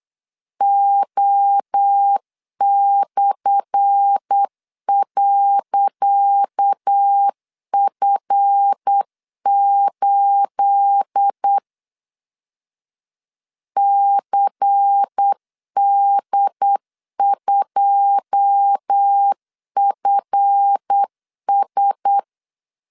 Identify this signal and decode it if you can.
Morse code, heard as a keyed tone, about 8 wpm, 800 Hz. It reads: O/.F8 CD2FS
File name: LEZIONI / 10 SIGNS / 10D COMA 8wpm 3749